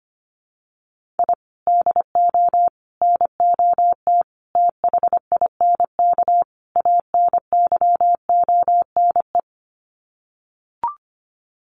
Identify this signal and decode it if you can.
Morse code, heard as a keyed tone, about 25 wpm, 700 Hz. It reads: I DO NOT THINK ANYONE
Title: Sets of 5 Words Encore 25wpm